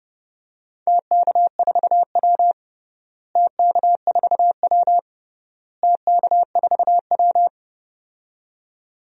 Morse code, heard as a keyed tone, about 31 wpm, 700 Hz. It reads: TK4W TK4W TK4W
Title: DXCC-100 Call Signs - 4 Characters - Repeated 3x - 30wpm